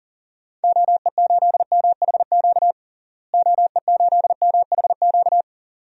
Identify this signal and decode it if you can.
Morse code, heard as a keyed tone, about 40 wpm, 700 Hz. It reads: OE8MHQ OE8MHQ
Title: DXCC-100 Call Signs - 6 Characters - Repeated 2x - 40wpm